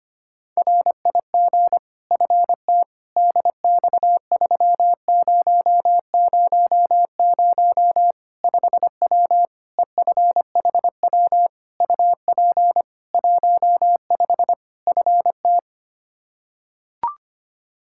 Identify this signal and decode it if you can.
Morse code, heard as a keyed tone, about 25 wpm, 700 Hz. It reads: RIG FTDX3000 5W EFHW UP 15 FT
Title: K1USN SST QSO Practice 25wpm